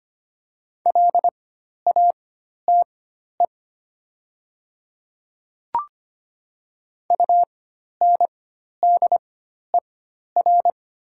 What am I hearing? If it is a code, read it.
Morse code, heard as a keyed tone, about 25 wpm, spaced as for 12 wpm, 700 Hz. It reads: LATE UNDER
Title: 4x Character Spacing - Top 300 Words - 25wpm